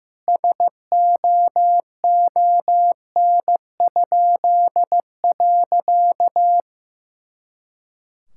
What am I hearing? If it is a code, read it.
Morse code, heard as a keyed tone, about 15 wpm, 700 Hz. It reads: SOON?.
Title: Pangram Challenge v2 - 15wpm